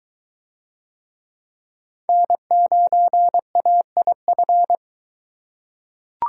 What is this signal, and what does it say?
Morse code, heard as a keyed tone, about 22 wpm, 700 Hz. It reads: N9AIF